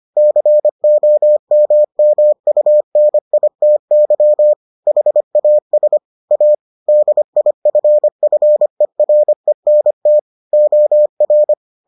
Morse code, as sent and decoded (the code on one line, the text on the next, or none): -.-. --- -- -- ..- -. .. - -.-- / .... .- ... / .- / -.. .. ..-. ..-. .-. -. - / --- .-.
COMMUNITY HAS A DIFFERENT OR